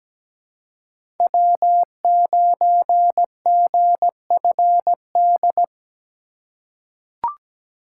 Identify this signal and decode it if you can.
Morse code, heard as a keyed tone, about 17 wpm, 700 Hz. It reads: W9GFD
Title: USA Call Signs - 5 Characters - 6x Character Spacing - 17wpm